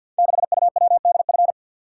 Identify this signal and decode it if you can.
Morse code, heard as a keyed tone, about 51 wpm, 700 Hz. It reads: 6UWDF
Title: Code Groups - 5 Letter:Numbers - 50wpm